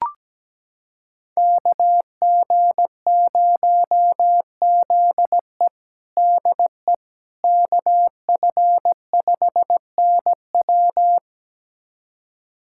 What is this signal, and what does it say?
Morse code, heard as a keyed tone, about 17 wpm, 700 Hz. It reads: KG0ZE DE KF5NW
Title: USA Callsign DE Callsign - 5 Characters - 17wpm